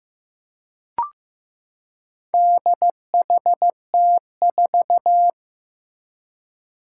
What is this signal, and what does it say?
Morse code, heard as a keyed tone, about 15 wpm, 700 Hz. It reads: DHT4
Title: Four Challenging Letter-Number Combinations 15wpm